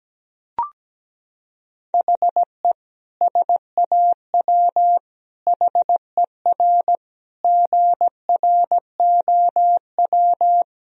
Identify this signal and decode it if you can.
Morse code, heard as a keyed tone, about 17 wpm, 700 Hz. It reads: HE SAW HER GROW
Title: Sentences from Top 1000 Words - Review Part 2 - 17wpm